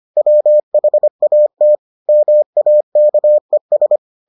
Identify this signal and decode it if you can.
Morse code, heard as a keyed tone, about 25 wpm, 600 Hz. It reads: WHAT MAKES